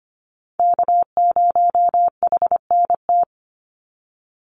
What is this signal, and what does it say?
Morse code, heard as a keyed tone, about 25 wpm, 700 Hz. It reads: K0HNT
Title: USA Call Signs 25wpm